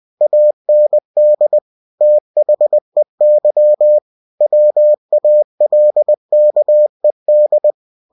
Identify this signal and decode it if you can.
Morse code, heard as a keyed tone, about 20 wpm, 600 Hz. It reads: AND THEY WALKED
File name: cw-books / winnie-the-pooh / winnie-20wpm-page-0084